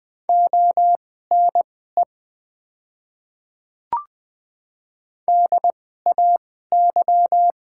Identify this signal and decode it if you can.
Morse code, heard as a keyed tone, about 20 wpm, spaced as for 15 wpm, 700 Hz. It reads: ONE DAY
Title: 2x Character Spacing - Top 100 Words - 20wpm